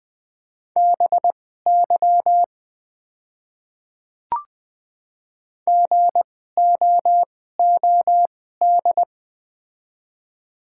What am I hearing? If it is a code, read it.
Morse code, heard as a keyed tone, about 20 wpm, spaced as for 14 wpm, 700 Hz. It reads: BY GOOD
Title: 2x Character Spacing - Top 300 Words - 20wpm